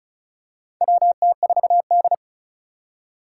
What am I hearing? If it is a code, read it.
Morse code, heard as a keyed tone, about 35 wpm, 700 Hz. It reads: WT4D
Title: USA Call Signs - 4 Characters - 2x Character Spacing - 35wpm